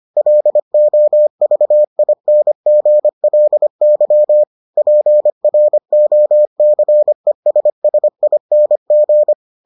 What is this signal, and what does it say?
Morse code, heard as a keyed tone, about 25 wpm, 600 Hz. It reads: LOVINGLY PROCESSING